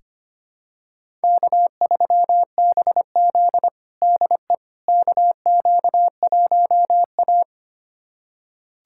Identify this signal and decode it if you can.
Morse code, heard as a keyed tone, about 25 wpm, 700 Hz. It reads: K3BZ DE KQ1A
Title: USA Callsign DE Callsign - 4 Characters - 25wpm